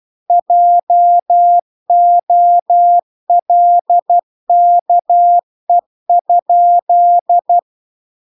Morse code, heard as a keyed tone, about 12 wpm, 700 Hz. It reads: JOLKE?